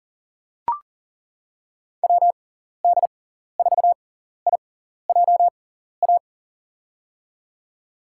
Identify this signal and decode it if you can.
Morse code, heard as a keyed tone, about 40 wpm, spaced as for 14 wpm, 700 Hz. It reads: WD4IJA
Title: USA Call Signs - 6 Characters - 6x Character Spacing - 40wpm